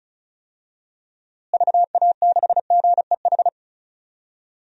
Morse code, heard as a keyed tone, about 34 wpm, 700 Hz. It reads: VA6GEH
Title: DXCC-100 Call Signs - 6 Characters - Repeated 3x - 35wpm